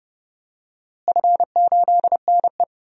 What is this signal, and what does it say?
Morse code, heard as a keyed tone, about 32 wpm, 700 Hz. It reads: F8NE
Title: DXCC-100 Call Signs - 4 Characters - Repeated 2x - 30wpm